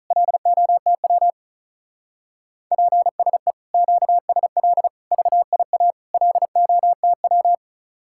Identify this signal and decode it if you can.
Morse code, heard as a keyed tone, about 44 wpm, 700 Hz. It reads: LOTW PSE QSL VIA LOTW
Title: Speed-Racing Common QSO Phrases - 35wpm